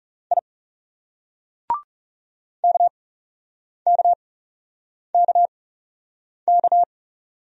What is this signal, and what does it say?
Morse code, heard as a keyed tone, about 37 wpm, spaced as for 8 wpm, 700 Hz. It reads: I KKKK
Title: Speed-Racing Single Letters 30wpm